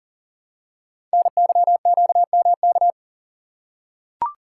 Morse code, heard as a keyed tone, about 41 wpm, 700 Hz. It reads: NYQMK
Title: Code Groups - 5 Letters - 40wpm